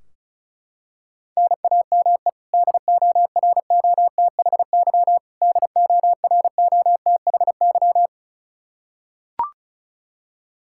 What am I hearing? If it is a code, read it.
Morse code, heard as a keyed tone, about 36 wpm, 700 Hz. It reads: NAME DOROTHY DOROTHY
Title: Practice QSO - Exchange 1 - Beginning - 35wpm